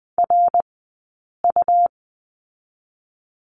Morse code, heard as a keyed tone, about 20 wpm, 700 Hz. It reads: R U
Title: 113 Lesson 19d - Letter w - Two Characters 20wpm